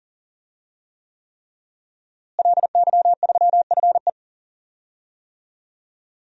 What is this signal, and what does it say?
Morse code, heard as a keyed tone, about 40 wpm, 700 Hz. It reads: LY3FE
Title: DXCC-100 Call Signs - 5 Characters - Repeated 3x - 40wpm